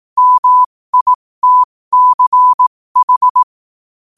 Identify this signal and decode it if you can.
Morse code, heard as a keyed tone, about 18 wpm, spaced as for 15 wpm, 1000 Hz. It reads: MITCH